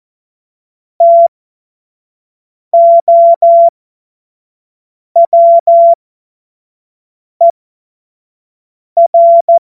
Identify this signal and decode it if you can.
Morse code, heard as a keyed tone, about 14 wpm, spaced as for 5 wpm, 680 Hz. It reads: TOWER
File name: Morsekurs / Witze / Witz 20. Juni 2023 25 F70